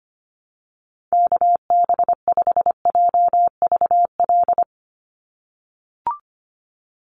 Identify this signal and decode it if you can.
Morse code, heard as a keyed tone, about 25 wpm, 700 Hz. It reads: KB5JVL
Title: USA Call Signs 25wpm